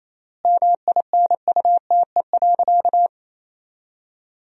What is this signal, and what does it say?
Morse code, heard as a keyed tone, about 28 wpm, 700 Hz. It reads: MINUTE.
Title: 3x Word-Spacing - Sentences from Top 500 Words - 28wpm